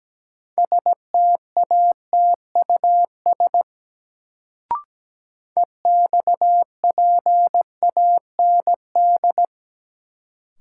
Morse code, heard as a keyed tone, about 17 wpm, 700 Hz. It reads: STATUS EXPAND
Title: Sentences from Top 1400 Words - New Word Drill - 17wpm